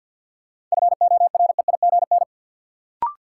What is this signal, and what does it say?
Morse code, heard as a keyed tone, about 52 wpm, 700 Hz. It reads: FOREIGN
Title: Sentences from Top 1400 Words - New Word Drill - 50wpm